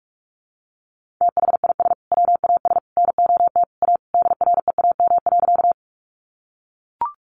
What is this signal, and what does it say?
Morse code, heard as a keyed tone, about 44 wpm, 700 Hz. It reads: THIS WAS NOT A DREAM.
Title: Sentences from Top 1500 Words 45wpm